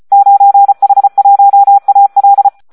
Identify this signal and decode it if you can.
Morse code, heard as a keyed tone, about 34 wpm, 800 Hz. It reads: 9H1AL
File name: Audio practica CW / SD 1 34